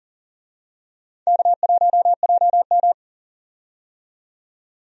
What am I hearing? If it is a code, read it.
Morse code, heard as a keyed tone, about 40 wpm, 700 Hz. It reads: K1JM
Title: USA Call Signs - 4 Characters - 40wpm